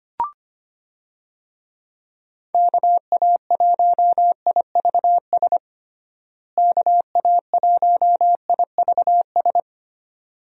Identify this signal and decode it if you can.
Morse code, heard as a keyed tone, about 25 wpm, 700 Hz. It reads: KA1IVS KA1IVS
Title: USA Call Signs - 6 Characters - Repeated 2x - 25wpm